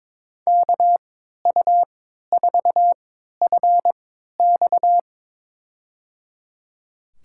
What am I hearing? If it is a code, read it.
Morse code, heard as a keyed tone, about 22 wpm, spaced as for 12 wpm, 700 Hz. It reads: KU4FX
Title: USA Call Signs - 5 Characters - 3x Character Spacing - 22wpm